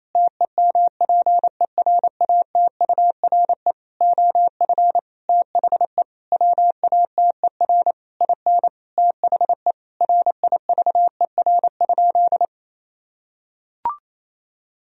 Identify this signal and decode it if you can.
Morse code, heard as a keyed tone, about 28 wpm, 700 Hz. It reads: TEMPERATURE OF THE WATER IN THE RIVER?
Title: Sentences from Top 1000 Words - Review Part 4 - 28wpm